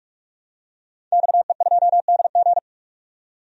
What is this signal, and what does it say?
Morse code, heard as a keyed tone, about 46 wpm, 700 Hz. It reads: XE2DG
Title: DXCC-100 Call Signs - 5 Characters - 45wpm